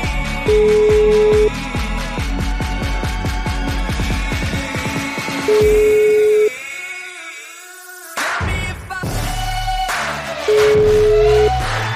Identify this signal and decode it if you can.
Ringback tone (Europe, 425 Hz), with noise.